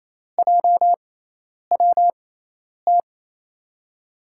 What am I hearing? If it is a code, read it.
Morse code, heard as a keyed tone, about 28 wpm, spaced as for 10 wpm, 700 Hz. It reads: JWT
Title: USA Call Signs - 6 Characters - 6x Character Spacing - 28wpm